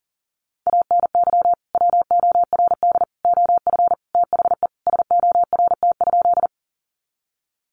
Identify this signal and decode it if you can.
Morse code, heard as a keyed tone, about 40 wpm, 700 Hz. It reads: ANY WORD OF THE SORT?
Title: Sentences from Top 1300 Words 40wpm